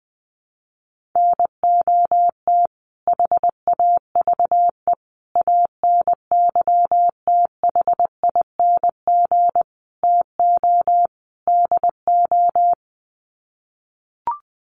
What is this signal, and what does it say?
Morse code, heard as a keyed tone, about 19 wpm, 700 Hz. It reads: NOT HAVE ANYTHING TO DO
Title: Sets of 5 Words Encore 20wpm